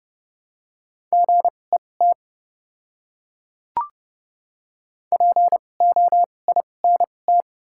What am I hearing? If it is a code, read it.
Morse code, heard as a keyed tone, about 30 wpm, spaced as for 22 wpm, 700 Hz. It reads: GET POINT
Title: 2x Character Spacing - Top 200 Words - 30wpm